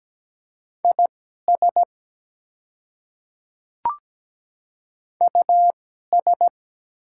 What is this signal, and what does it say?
Morse code, heard as a keyed tone, about 17 wpm, spaced as for 12 wpm, 700 Hz. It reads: IS US
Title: 2x Character Spacing - Top 300 Words - 17wpm